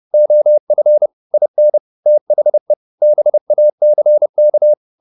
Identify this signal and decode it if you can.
Morse code, heard as a keyed tone, about 30 wpm, 600 Hz. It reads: OF IN THE BACK